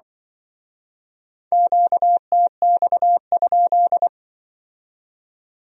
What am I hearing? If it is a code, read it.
Morse code, heard as a keyed tone, about 24 wpm, 700 Hz. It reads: QTX?
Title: Speed-Racing Q-Codes - 20wpm